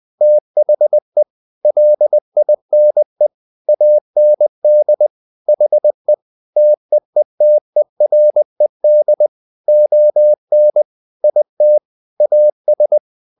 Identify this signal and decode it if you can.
Morse code, heard as a keyed tone, about 20 wpm, 600 Hz. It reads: THE LINE AND HE TEETERED ON IT AS
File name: the old man and the sea / the old man and the sea part 1 20wpm